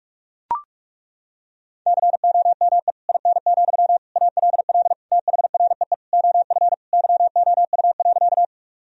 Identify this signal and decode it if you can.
Morse code, heard as a keyed tone, about 41 wpm, 700 Hz. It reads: COME IN, ALL THREE OF YOU.
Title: Sentences from Top 1000 Words - Review Part 1 - 45wpm